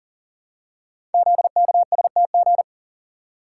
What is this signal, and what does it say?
Morse code, heard as a keyed tone, about 38 wpm, 700 Hz. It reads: ZKSTG
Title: Code Groups - 5 Letters - 40wpm